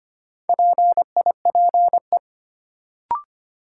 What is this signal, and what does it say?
Morse code, heard as a keyed tone, about 25 wpm, 700 Hz. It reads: PIPE
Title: Sentences from Top 1500 Words - New Word Drill - 25wpm